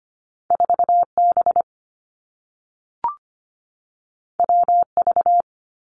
Morse code, heard as a keyed tone, about 25 wpm, 700 Hz. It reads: 4B WV
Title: Two Challenging Letter-Number Combinations 25wpm